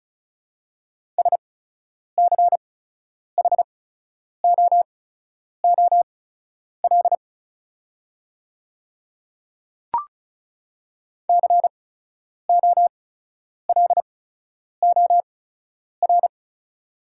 Morse code, heard as a keyed tone, about 36 wpm, spaced as for 10 wpm, 700 Hz. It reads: SCHOOL COLOR
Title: 8x Character Spacing - Top 300 Words - 35wpm